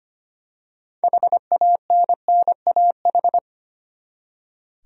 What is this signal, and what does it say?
Morse code, heard as a keyed tone, about 25 wpm, 700 Hz. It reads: HANNAH
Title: USA Common Names - 25wpm